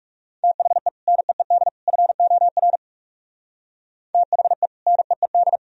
Morse code, heard as a keyed tone, about 49 wpm, 700 Hz. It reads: THE NEED FOR THE NEED